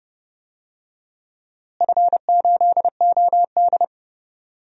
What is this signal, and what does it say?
Morse code, heard as a keyed tone, about 29 wpm, 700 Hz. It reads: F8OD
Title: DXCC-100 Call Signs - 4 Characters - Repeated 2x - 30wpm